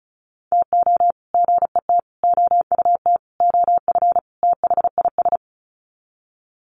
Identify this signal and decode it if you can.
Morse code, heard as a keyed tone, about 34 wpm, 700 Hz. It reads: TO GET OUT OF THIS
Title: Sets of 5 Words Encore2 35wpm